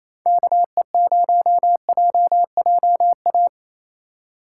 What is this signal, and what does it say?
Morse code, heard as a keyed tone, about 28 wpm, 700 Hz. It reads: KE0JJA
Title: USA Call Signs - 2x Character Spacing - 28wpm